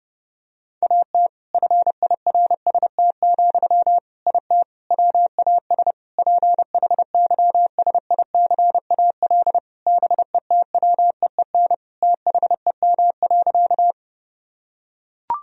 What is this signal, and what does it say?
Morse code, heard as a keyed tone, about 30 wpm, 700 Hz. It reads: AT FIRST, IT WAS PHYSICAL BETWEEN THEM.
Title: Sentences from Top 1000 Words - Review Part 4 - 30wpm